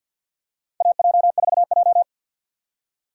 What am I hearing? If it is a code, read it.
Morse code, heard as a keyed tone, about 50 wpm, 690 Hz. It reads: AJ4J
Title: USA Call Signs - 4 Characters - 6x Character Spacing - 50wpm